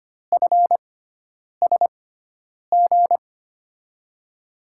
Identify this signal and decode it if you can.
Morse code, heard as a keyed tone, about 25 wpm, spaced as for 9 wpm, 700 Hz. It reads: FSG
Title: USA Call Signs - 6 Characters - 6x Character Spacing - 25wpm